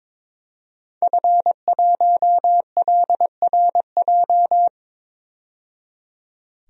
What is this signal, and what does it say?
Morse code, heard as a keyed tone, about 22 wpm, 700 Hz. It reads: F1LRJ
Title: DXCC-100 Call Signs - 5 Characters - 22wpm